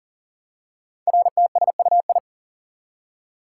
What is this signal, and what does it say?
Morse code, heard as a keyed tone, about 42 wpm, 690 Hz. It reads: RTSUI
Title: Code Groups - 5 Letters - 40wpm